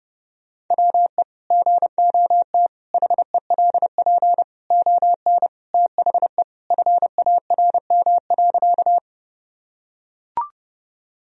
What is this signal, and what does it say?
Morse code, heard as a keyed tone, about 30 wpm, 700 Hz. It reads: WE GOT HELP ON THE FARM.
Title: Sentences from Top 1000 Words - Review Part 2 - 30wpm